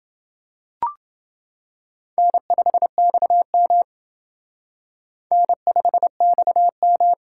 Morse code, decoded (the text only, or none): N5XM N5XM